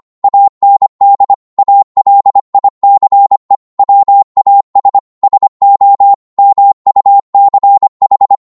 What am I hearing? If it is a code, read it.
Morse code, heard as a keyed tone, about 25 wpm, 800 Hz. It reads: AND ALICE WAS SO MUCH
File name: 25wpm